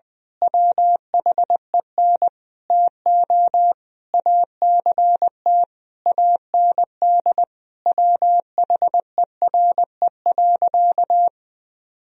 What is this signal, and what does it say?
Morse code, heard as a keyed tone, about 20 wpm, 700 Hz. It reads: WHEN TO ACT AND WHERE.
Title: Sentences from Top 1000 Words - Review Part 1 - 20wpm